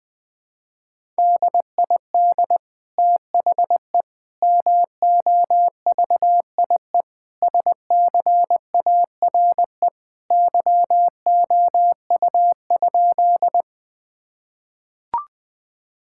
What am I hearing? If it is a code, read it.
Morse code, heard as a keyed tone, about 20 wpm, 700 Hz. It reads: DID THE MOVIE SCARE YOU?
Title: Sentences from Top 1000 Words - Review Part 5 - 20wpm